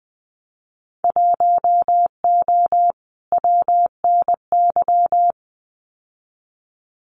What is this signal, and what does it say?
Morse code, heard as a keyed tone, about 20 wpm, 700 Hz. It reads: 1O WNY
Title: Winter Field Day Exchanges 20wpm